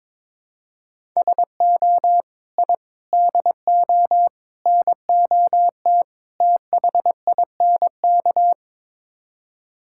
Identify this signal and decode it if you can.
Morse code, heard as a keyed tone, about 22 wpm, 700 Hz. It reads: SO I DO NOT THINK
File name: practice / 22wpm / Build A Phrase - 5-Word Phrases - 22wpm